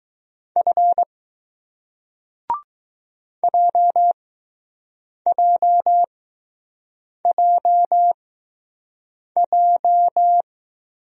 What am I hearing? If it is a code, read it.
Morse code, heard as a keyed tone, about 20 wpm, spaced as for 6 wpm, 700 Hz. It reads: F JJJJ